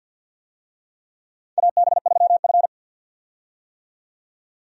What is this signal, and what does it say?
Morse code, heard as a keyed tone, about 51 wpm, 700 Hz. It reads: AB3F
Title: USA Call Signs - 4 Characters - 50wpm